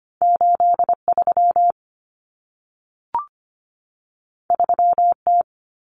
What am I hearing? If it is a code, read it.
Morse code, heard as a keyed tone, about 25 wpm, 700 Hz. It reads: 83 3T
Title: Common Cut Numbers - 2-Digit 25wpm